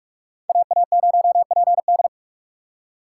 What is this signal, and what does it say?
Morse code, heard as a keyed tone, about 46 wpm, 700 Hz. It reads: AA0PD
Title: USA Call Signs - 6x Character Spacing - 45wpm